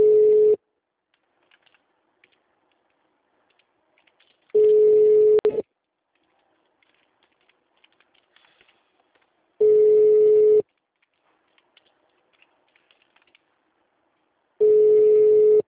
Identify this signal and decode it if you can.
Ringback tone (Europe, 425 Hz).
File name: out- -2008- - - .1102